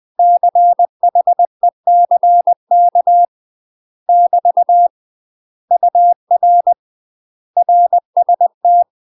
Morse code, heard as a keyed tone, about 20 wpm, 700 Hz. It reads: CHECK <BT> UR RST
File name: UK MASTER QSOs TRAINING/20WPM